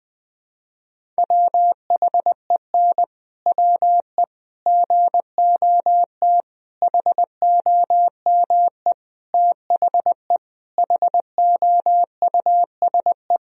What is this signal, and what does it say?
Morse code, heard as a keyed tone, about 20 wpm, 700 Hz. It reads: WHEN WE GOT HOME THE HOUSE
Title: Sentences from Top 500 Words and Abbreviations 20wpm